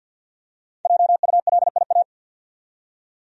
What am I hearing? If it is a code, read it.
Morse code, heard as a keyed tone, about 49 wpm, 700 Hz. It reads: JULIA
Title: USA Common Names - Max 5 Letters - 50wpm